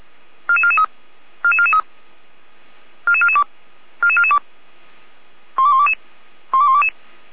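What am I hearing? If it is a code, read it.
Five-tone selective call (ZVEI), heard as quick runs of five tones, tones of about 70 ms. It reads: 44443 44443 44442 44442 12122 12122
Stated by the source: ZVEI five-tone call 44441 received by SDR.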